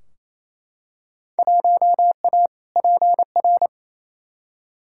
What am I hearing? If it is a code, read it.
Morse code, heard as a keyed tone, about 28 wpm, 700 Hz. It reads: 1A PR
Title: Field Day Exchanges Encore - 28wpm